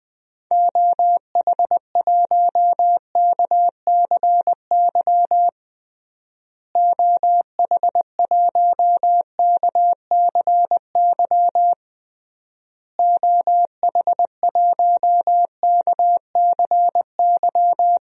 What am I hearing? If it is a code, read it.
Morse code, heard as a keyed tone, about 20 wpm, 700 Hz. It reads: OH1KCY OH1KCY OH1KCY
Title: DXCC-100 Call Signs - 6 Characters - Repeated 3x - 20wpm